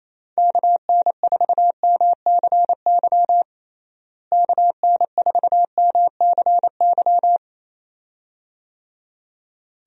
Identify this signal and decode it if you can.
Morse code, heard as a keyed tone, about 28 wpm, 700 Hz. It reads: KN4MCY KN4MCY
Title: USA Call Signs - 6 Characters - Repeated 2x - 28wpm